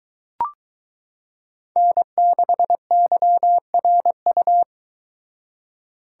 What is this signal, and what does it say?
Morse code, heard as a keyed tone, about 22 wpm, 700 Hz. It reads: N6YRU